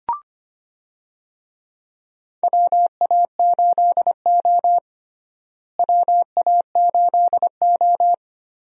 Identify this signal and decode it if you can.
Morse code, heard as a keyed tone, about 25 wpm, 700 Hz. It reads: WA8O WA8O
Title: USA Call Signs - 4 Characters - Repeated 3x - 25wpm